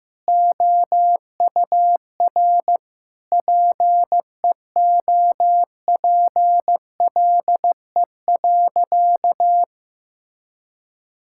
Morse code, heard as a keyed tone, about 15 wpm, 700 Hz. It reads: OUR PEOPLE.
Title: Sentences from Top 1000 Words - Review Part 5 - 15wpm